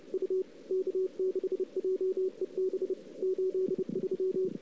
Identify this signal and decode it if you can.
Morse code, heard as a keyed tone, about 33 wpm, 380 Hz. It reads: U K6JEB 83
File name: CW / 2008-ss-cw-k5tr-op-n5rz -00:00